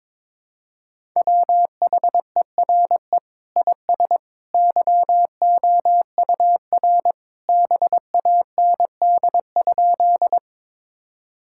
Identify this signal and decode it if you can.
Morse code, heard as a keyed tone, about 22 wpm, 700 Hz. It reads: WHERE IS YOUR BAND?
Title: Sentences from Top 1600 Words 22wpm